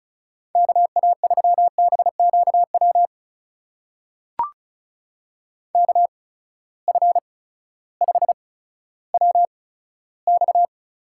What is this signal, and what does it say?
Morse code, heard as a keyed tone, about 35 wpm, spaced as for 16 wpm, 700 Hz. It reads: KA3BQW KF5WX
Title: USA Call Signs - 6 Characters - 8x Character Spacing - 35wpm